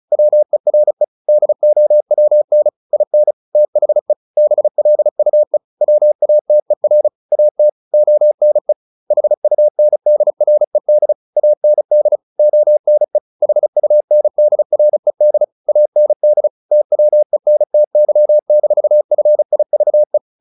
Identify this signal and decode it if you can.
Morse code, heard as a keyed tone, about 34 wpm, 600 Hz. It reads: WERE DOWN IN THE BLUE WATER AT ONE HUNDRED AND ONE HUNDRED AND TWENTY-FIVE
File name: the old man and the sea / the old man and the sea 1 35wpm